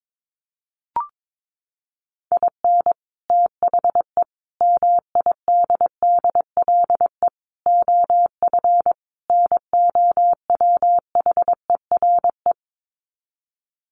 Morse code, heard as a keyed tone, about 22 wpm, 700 Hz. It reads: IN THE MIDDLE OF NOWHERE
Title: Sets of 5 Words Encore 22wpm